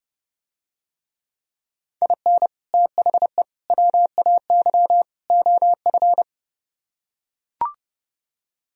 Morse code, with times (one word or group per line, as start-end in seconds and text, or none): 2.0-2.5 IN
2.7-3.4 THE
3.7-5.0 WAY
5.3-6.2 OF